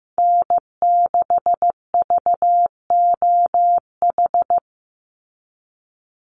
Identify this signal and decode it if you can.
Morse code, heard as a keyed tone, about 15 wpm, 700 Hz. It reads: N6VOH
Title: Call Signs Repeated 3x - 15wpm